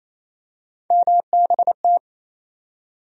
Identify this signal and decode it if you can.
Morse code, heard as a keyed tone, about 28 wpm, 700 Hz. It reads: MBT
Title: USA Call Sign Postfixes - 3 Letters 28wpm